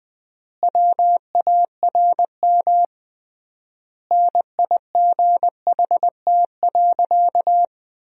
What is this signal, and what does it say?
Morse code, heard as a keyed tone, about 20 wpm, 700 Hz. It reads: WARM NIGHT.
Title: 3x Word-Spacing - Sentences from Top 500 Words - 20wpm